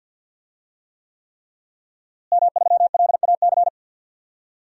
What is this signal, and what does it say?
Morse code, heard as a keyed tone, about 50 wpm, 700 Hz. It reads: M3LAC
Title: DXCC-100 Call Signs - 5 Characters - Repeated 3x - 50wpm